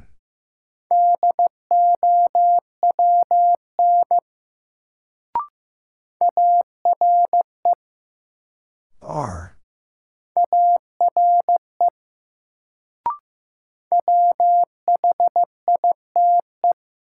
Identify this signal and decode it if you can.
Morse code, heard as a keyed tone, about 15 wpm, 700 Hz. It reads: DOWN ARE ARE WHITE
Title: Sets of 2 Words - New Word Drill - 15wpm